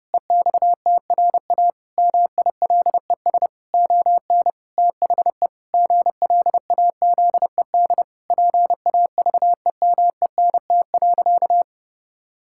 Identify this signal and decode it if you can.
Morse code, heard as a keyed tone, about 30 wpm, 700 Hz. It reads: EXTRA MILES ON THE GLAZED PAVEMENT.
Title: Pangram Challenge v2 - 30wpm